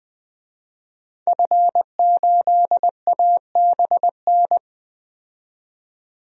Morse code, heard as a keyed tone, about 21 wpm, 700 Hz. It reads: F8ABN
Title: DXCC-100 Call Signs - 5 Characters - 20wpm